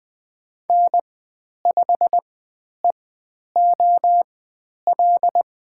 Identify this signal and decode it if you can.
Morse code, heard as a keyed tone, about 20 wpm, spaced as for 10 wpm, 700 Hz. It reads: N5EOL
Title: 109 Lesson 18g - Letter m - Call Signs 20wpm